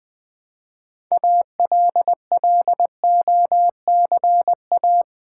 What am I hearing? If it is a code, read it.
Morse code, heard as a keyed tone, about 20 wpm, 700 Hz. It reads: ALLOCA
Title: Build A Word - 3 Letters At A Time - 8-Letter Words - 20wpm